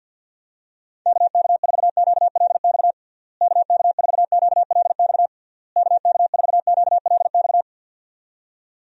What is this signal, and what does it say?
Morse code, heard as a keyed tone, about 50 wpm, 700 Hz. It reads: KK4QLX KK4QLX KK4QLX
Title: USA Call Signs - 6 Characters - Repeated 3x - 50wpm